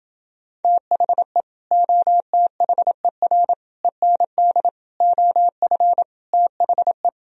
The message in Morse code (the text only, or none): THE OTHER END OF THE